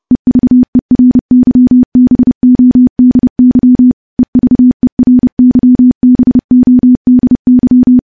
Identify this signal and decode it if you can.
Morse code, heard as a keyed tone, about 30 wpm, 250 Hz. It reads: EVERYBODY EVERYBODY
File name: morse / morse-training-30WPM